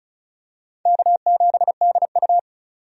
Morse code, heard as a keyed tone, about 36 wpm, 700 Hz. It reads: K7DU
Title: USA Call Signs - 4 Characters - 35wpm